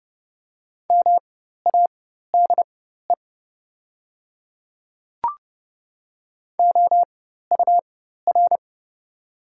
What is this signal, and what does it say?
Morse code, heard as a keyed tone, about 30 wpm, spaced as for 14 wpm, 690 Hz. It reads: MADE OUR
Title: 4x Character Spacing - Top 500 Words - 30wpm